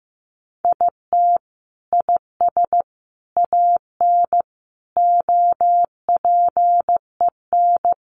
Morse code, heard as a keyed tone, about 15 wpm, 700 Hz. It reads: IT IS AN OPEN
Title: Sentences from Top 1300 Words 15wpm